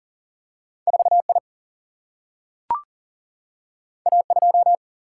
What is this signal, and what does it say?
Morse code, heard as a keyed tone, about 40 wpm, 700 Hz. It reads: 4 I A 2
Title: Single-Letter Single-Number 40wpm